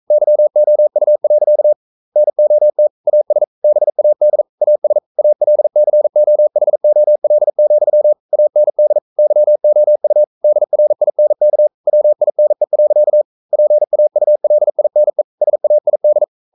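Morse code, heard as a keyed tone, about 42 wpm, 600 Hz. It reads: YOU. NOT AS BAD AS ALCOHOL, AND YOU DRINK WINE. PAULINE SAID
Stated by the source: Morse code sending of an English novel at 42 wpm.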